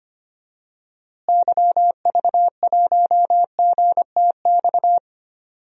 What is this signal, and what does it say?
Morse code, heard as a keyed tone, about 25 wpm, 700 Hz. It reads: YV1GTX
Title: DXCC-100 Call Signs - 6 Characters - Repeated 2x - 25wpm